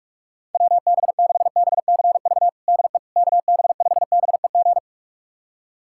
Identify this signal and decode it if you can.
Morse code, heard as a keyed tone, about 42 wpm, 700 Hz. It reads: WB6BCV DE KB5BEG
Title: USA Callsign DE Callsign - 6 Characters - 45wpm